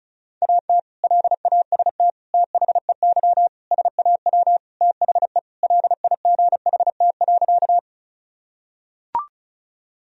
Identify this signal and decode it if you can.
Morse code, heard as a keyed tone, about 34 wpm, 700 Hz. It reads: AT LAST THEY SAW THE LIGHT.
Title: Sentences from Top 1000 Words - Review Part 1 - 35wpm